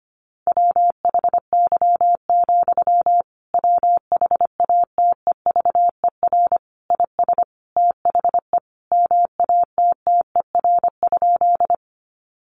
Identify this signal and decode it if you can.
Morse code, heard as a keyed tone, about 25 wpm, 700 Hz. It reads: WHY, WHATEVER IS THE MATTER?
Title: Sentences from Top 1100 Words 25wpm